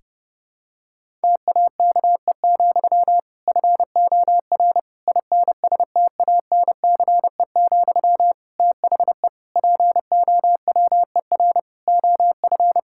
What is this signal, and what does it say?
Morse code, heard as a keyed tone, about 30 wpm, 700 Hz. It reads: TAKE, FOR INSTANCE, THE POWER OF
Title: Sentences from Top 1000 Words - Review Part 5 - 30wpm